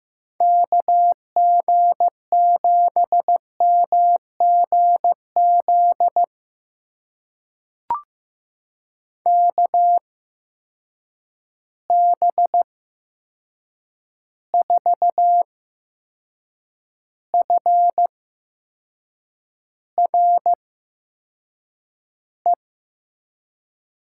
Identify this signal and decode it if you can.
Morse code, heard as a keyed tone, about 15 wpm, spaced as for 6 wpm, 700 Hz. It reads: KG7MGZ KB4FRE
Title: USA Call Signs - 6 Characters - 8x Character Spacing - 15wpm